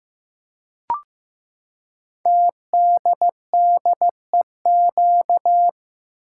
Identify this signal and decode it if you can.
Morse code, heard as a keyed tone, about 15 wpm, 700 Hz. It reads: TDDEQ
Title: Code Groups - 5 Letters - 15wpm